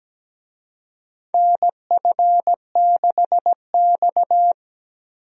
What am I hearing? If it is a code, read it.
Morse code, heard as a keyed tone, about 17 wpm, 700 Hz. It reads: NF6X
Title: USA Call Signs - 4 Characters - 17wpm